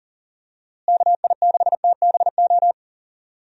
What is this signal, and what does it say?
Morse code, heard as a keyed tone, about 37 wpm, 700 Hz. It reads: KI6TBO
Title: USA Call Signs - 6 Characters - Repeated 3x - 40wpm